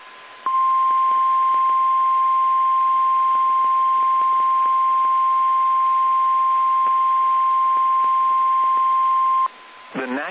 Emergency Alert System (SAME) tones (the 1050 Hz attention tone). It is heard from 0.4 to 9.5 s.
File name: eas / KSDZKDJL / KSDZKDJL